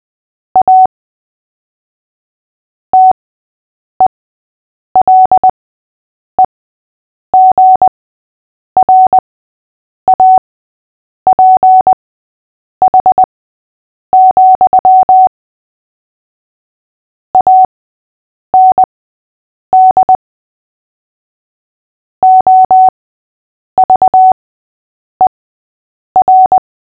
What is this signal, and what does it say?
Morse code, heard as a keyed tone, about 20 wpm, spaced as for 8 wpm, 750 Hz. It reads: A TELEGRAPH, AND OVER
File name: cw / fists / Short-Story-6-8wpm